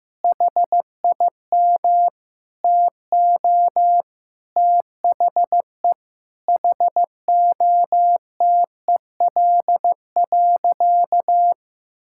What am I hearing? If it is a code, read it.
Morse code, heard as a keyed tone, about 15 wpm, 700 Hz. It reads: HIM TO THE HOTEL.
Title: Sentences from Top 1600 Words 15wpm